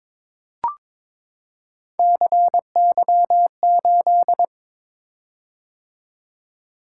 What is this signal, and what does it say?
Morse code, heard as a keyed tone, about 23 wpm, 700 Hz. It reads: CY8
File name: practice / 22wpm / Three Letter-Number Combinations - 22wpm